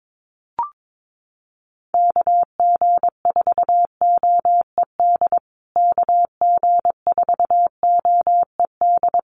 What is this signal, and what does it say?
Morse code, heard as a keyed tone, about 22 wpm, 700 Hz. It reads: KG4OED KG4OED
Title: USA Call Signs Repeated 3x 22wpm